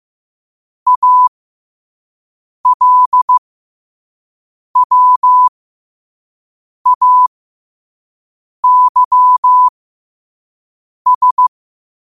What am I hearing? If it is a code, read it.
Morse code, heard as a keyed tone, about 15 wpm, spaced as for 5 wpm, 1000 Hz. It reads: ALWAYS
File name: qotdfiles / 5wpm